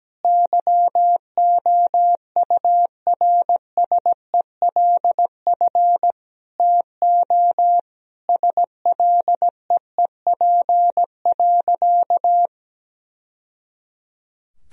Morse code, heard as a keyed tone, about 17 wpm, 700 Hz. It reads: YOURSELF TO SLEEP.
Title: Sentences from Top 1000 Words - Review Part 2 - 17wpm